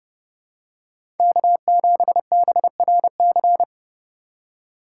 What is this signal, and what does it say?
Morse code, heard as a keyed tone, about 32 wpm, 700 Hz. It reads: K7BRC
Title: USA Call Signs - 5 Characters - 2x Character Spacing - 30wpm